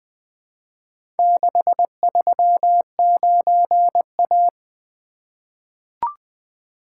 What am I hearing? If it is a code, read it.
Morse code, heard as a keyed tone, about 20 wpm, 700 Hz. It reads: 639A
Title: Four Letter-Number Combinations - 20wpm